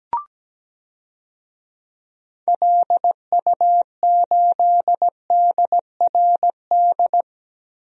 Morse code, heard as a keyed tone, about 17 wpm, 700 Hz. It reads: LU8DRD
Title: DXCC-100 Call Signs - 6 Characters - Repeated 3x - 17wpm